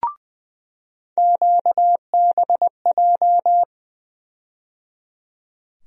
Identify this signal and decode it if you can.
Morse code, heard as a keyed tone, about 20 wpm, 700 Hz. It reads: QBJ
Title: USA Call Sign Postfixes - 3 Letters 20wpm